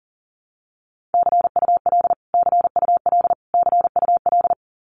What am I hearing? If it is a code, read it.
Morse code, heard as a keyed tone, about 40 wpm, 700 Hz. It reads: CUL CUL CUL
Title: Abbreviations Repeated 3x - 40wpm